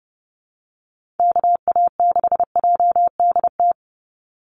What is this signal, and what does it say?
Morse code, heard as a keyed tone, about 30 wpm, 700 Hz. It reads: KA6JDT
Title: USA Call Signs 30wpm